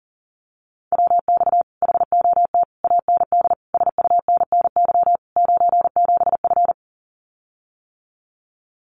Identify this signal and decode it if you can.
Morse code, heard as a keyed tone, about 40 wpm, 700 Hz. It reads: WX HOT AND SUNNY 97F
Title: Common QSO Phrases Encore 40wpm